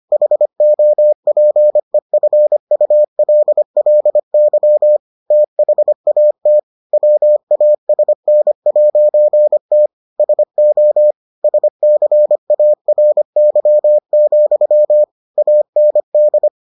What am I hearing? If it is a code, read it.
Morse code, heard as a keyed tone, about 25 wpm, 600 Hz. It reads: HOPEFULLY THAT WASN'T SO SCARY, AND